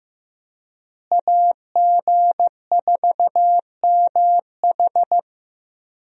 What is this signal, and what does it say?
Morse code, heard as a keyed tone, about 16 wpm, 700 Hz. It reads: AG4MH
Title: USA Call Signs - 5 Characters - Repeated 3x - 15wpm